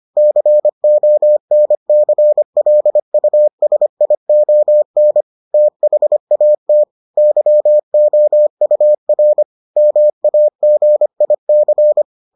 Morse code, heard as a keyed tone, about 25 wpm, 600 Hz. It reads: CONCLUSION THAT YOUR MAGIC